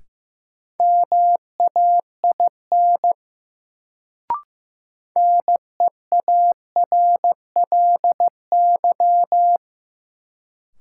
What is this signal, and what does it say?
Morse code, heard as a keyed tone, about 15 wpm, 700 Hz. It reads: MAIN NEARLY
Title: Sentences from Top 1300 Words - New Word Drill Alt - 15wpm